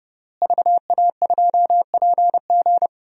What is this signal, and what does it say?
Morse code, heard as a keyed tone, about 30 wpm, 700 Hz. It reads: VA2PG
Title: DXCC-100 Call Signs - 5 Characters - Repeated 3x - 30wpm